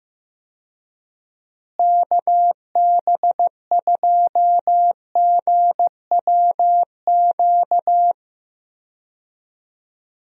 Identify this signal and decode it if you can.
Morse code, heard as a keyed tone, about 15 wpm, 700 Hz. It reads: KB2GWQ
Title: USA Call Signs - 6 Characters - Repeated 3x - 15wpm